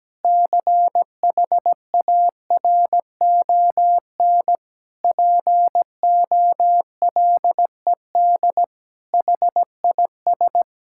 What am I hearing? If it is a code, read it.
Morse code, heard as a keyed tone, about 17 wpm, 700 Hz. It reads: CHARON POLED HIS
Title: Pangram Challenge v2 - 17wpm